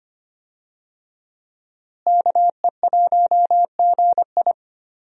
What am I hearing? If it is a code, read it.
Morse code, heard as a keyed tone, about 25 wpm, 700 Hz. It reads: KE1GI
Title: USA Call Signs - 5 Characters - Repeated 2x - 25wpm